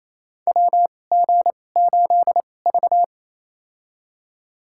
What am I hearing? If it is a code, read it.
Morse code, heard as a keyed tone, about 28 wpm, spaced as for 20 wpm, 700 Hz. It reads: WG8V
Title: USA Call Signs - 2x Character Spacing - 28wpm